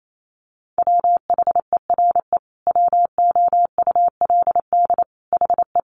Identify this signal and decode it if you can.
Morse code, heard as a keyed tone, about 28 wpm, 700 Hz. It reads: WHERE WOULD HE
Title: Sentences from Top 1300 Words 28wpm